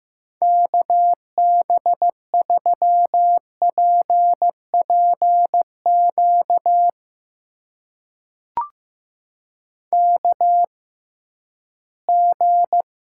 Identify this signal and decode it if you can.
Morse code, heard as a keyed tone, about 15 wpm, spaced as for 11 wpm, 700 Hz. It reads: KB3PPQ KG
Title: USA Call Signs - 6 Characters - 6x Character Spacing - 15wpm